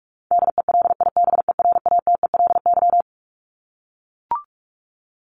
Morse code, heard as a keyed tone, about 44 wpm, 700 Hz. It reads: DELIBERATELY
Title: Top 1751-2000 Words 45wpm